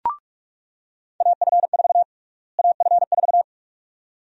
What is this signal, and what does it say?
Morse code, heard as a keyed tone, about 46 wpm, 700 Hz. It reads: AF4 AF4
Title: USA Call Sign Prefixes - Repeated 2x - 45wpm